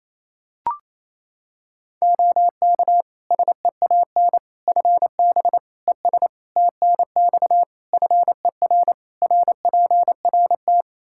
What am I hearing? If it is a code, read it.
Morse code, heard as a keyed tone, about 28 wpm, 700 Hz. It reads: OK SEAN FB ES TNX FER RPRT
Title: Practice QSO - Exchange 2 - Info - 28wpm